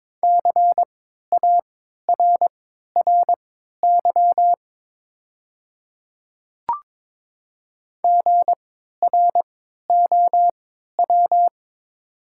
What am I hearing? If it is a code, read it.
Morse code, heard as a keyed tone, about 23 wpm, spaced as for 13 wpm, 690 Hz. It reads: CARRY GROW
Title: 3x Character Spacing - Top 300 Words - 22wpm